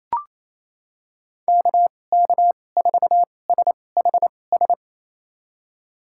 Morse code, heard as a keyed tone, about 28 wpm, spaced as for 20 wpm, 700 Hz. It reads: KK4SHS
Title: USA Call Signs - 6 Characters - 2x Character Spacing - 28wpm